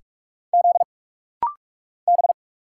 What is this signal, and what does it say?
Morse code, heard as a keyed tone, about 46 wpm, 700 Hz. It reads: Z B
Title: Single Letter-Number Alt - Rapid-Fire 45wpm